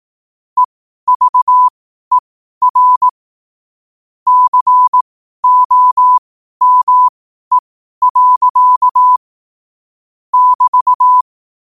Morse code, heard as a keyed tone, about 18 wpm, spaced as for 12 wpm, 1000 Hz. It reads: EVER COME. <BT>
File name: qotdfiles / 12wpm